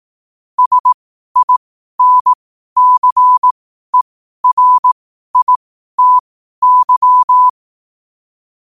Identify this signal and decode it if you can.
Morse code, heard as a keyed tone, about 18 wpm, spaced as for 12 wpm, 1000 Hz. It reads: SINCERITY
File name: qotdfiles / 12wpm